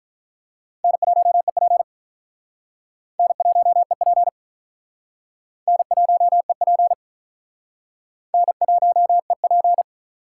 Morse code, heard as a keyed tone, about 43 wpm, 700 Hz. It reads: N1EP N1EP N1EP N1EP
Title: Speed Racing - USA 1x2 Call Signs 35wpm